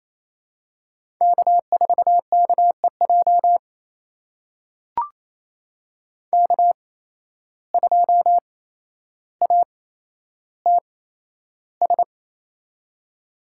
Morse code, heard as a keyed tone, about 28 wpm, spaced as for 12 wpm, 700 Hz. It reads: K4KEJ K2ATS
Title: USA Call Signs - 5 Characters - 8x Character Spacing - 28wpm